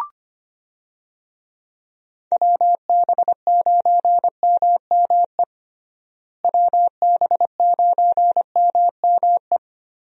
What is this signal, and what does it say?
Morse code, heard as a keyed tone, about 25 wpm, 700 Hz. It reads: WB9MME WB9MME
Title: USA Call Signs - 6 Characters - Repeated 3x - 25wpm